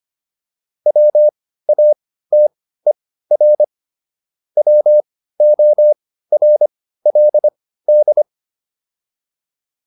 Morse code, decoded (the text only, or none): WATER WORLD